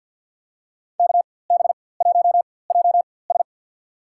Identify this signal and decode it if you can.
Morse code, heard as a keyed tone, about 51 wpm, spaced as for 23 wpm, 700 Hz. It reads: KB1JS
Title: USA Call Signs - 5 Characters - 4x Character Spacing - 50wpm